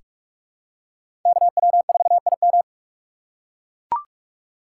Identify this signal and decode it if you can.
Morse code, heard as a keyed tone, about 44 wpm, 690 Hz. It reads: KW4IM
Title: USA Call Signs - 5 Characters - 6x Character Spacing - 45wpm